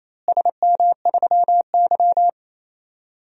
Morse code, heard as a keyed tone, about 28 wpm, 700 Hz. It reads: SM3Y
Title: DXCC-100 Call Signs - 4 Characters - Repeated 2x - 28wpm